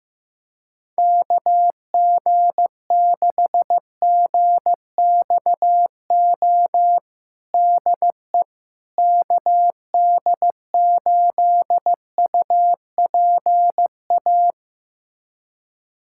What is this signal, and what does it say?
Morse code, heard as a keyed tone, about 15 wpm, 700 Hz. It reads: KG6GXO DE KD8UPA
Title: USA Callsign DE Callsign - 6 Characters - 15wpm